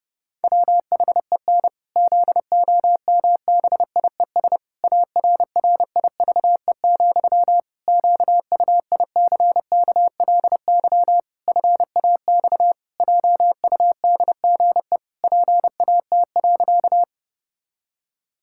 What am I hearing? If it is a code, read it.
Morse code, heard as a keyed tone, about 30 wpm, 700 Hz. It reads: WHEN ZOMBIES ARRIVE, QUICKLY FAX JUDGE PAT.
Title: Pangram Challenge v2 - 30wpm